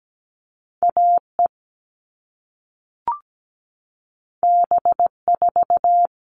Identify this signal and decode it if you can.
Morse code, heard as a keyed tone, about 17 wpm, 700 Hz. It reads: AE B4
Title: All Cut Numbers - 2-Digit 17wpm